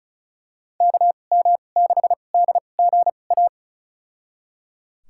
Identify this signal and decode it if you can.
Morse code, heard as a keyed tone, about 36 wpm, spaced as for 25 wpm, 700 Hz. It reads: KM6DGA
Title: USA Call Signs - 6 Characters - 2x Character Spacing - 35wpm